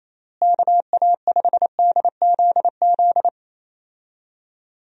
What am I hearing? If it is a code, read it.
Morse code, heard as a keyed tone, about 28 wpm, 700 Hz. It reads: KA5DZZ
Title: USA Call Signs - 6 Characters - Repeated 2x - 28wpm